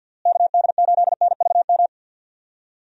Morse code, heard as a keyed tone, about 52 wpm, 700 Hz. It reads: KD8NVM
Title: USA Call Signs - 6 Characters - Repeated 2x - 50wpm